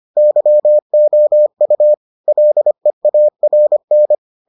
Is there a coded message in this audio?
Yes: Morse code from 0.2 to 4.1 s, heard as a keyed tone, about 25 wpm, 600 Hz. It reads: YOU LEARN